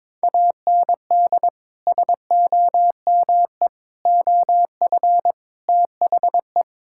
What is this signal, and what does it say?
Morse code, heard as a keyed tone, about 22 wpm, 700 Hz. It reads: AND SOME OF THE
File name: practice / 22wpm / Sentences with 4-Word Phrases - 22wpm